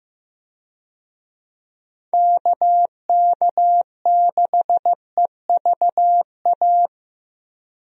Morse code, heard as a keyed tone, about 15 wpm, 700 Hz. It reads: KK6EVA
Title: USA Call Signs - 6 Characters - Repeated 2x - 15wpm